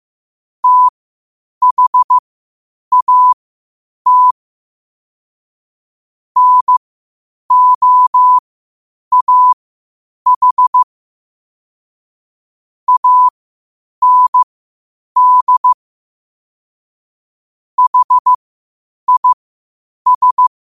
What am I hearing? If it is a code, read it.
Morse code, heard as a keyed tone, about 15 wpm, spaced as for 8 wpm, 1000 Hz. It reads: THAT NOAH AND HIS
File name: 8wpm